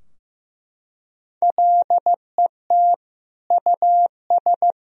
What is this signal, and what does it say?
Morse code, heard as a keyed tone, about 16 wpm, 700 Hz. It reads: LET US
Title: Sentences from Top 1000 Words - Review Part 3 - 15wpm